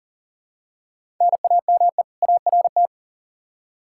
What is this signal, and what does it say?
Morse code, heard as a keyed tone, about 38 wpm, 700 Hz. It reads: NAME ART
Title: K1USN SST QSO Practice 40wpm